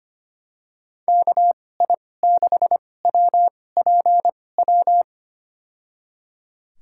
Morse code, heard as a keyed tone, about 25 wpm, spaced as for 18 wpm, 700 Hz. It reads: KI6WPW
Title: USA Call Signs - 2x Character Spacing - 25wpm